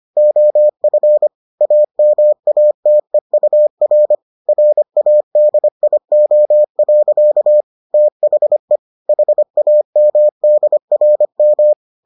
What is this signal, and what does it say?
Morse code, heard as a keyed tone, about 25 wpm, 600 Hz. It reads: OF AMATEUR RADIO. THE HAMDRM